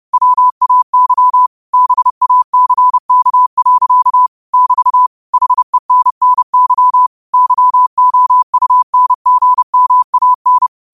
Morse code, heard as a keyed tone, about 30 wpm, 1000 Hz. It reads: WAY BACK. <BT> HENNY YOUNGMAN
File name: qotdfiles / 30wpm